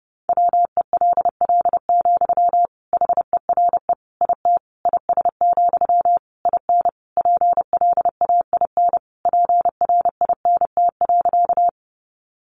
Morse code, heard as a keyed tone, about 30 wpm, 700 Hz. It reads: WELL, HERE IT IS, IN PLAIN PRINT.
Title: Sentences from Top 1400 Words 30wpm